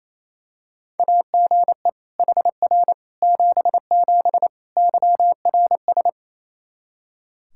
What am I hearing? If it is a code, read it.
Morse code, heard as a keyed tone, about 27 wpm, 700 Hz. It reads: AGE HR 77 YRS
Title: Practice QSO - Exchange 3 - More Info - 28wpm